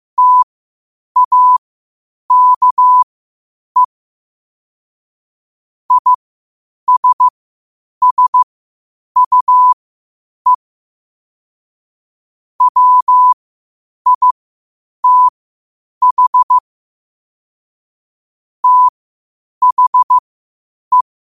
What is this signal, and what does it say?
Morse code, heard as a keyed tone, about 15 wpm, spaced as for 8 wpm, 1000 Hz. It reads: TAKE ISSUE WITH THE
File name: qotdfiles / 8wpm